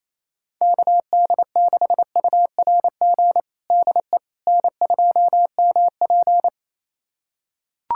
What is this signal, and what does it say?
Morse code, heard as a keyed tone, about 28 wpm, 700 Hz. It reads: KD6URG DE N2MP
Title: USA Callsign DE Callsign 28wpm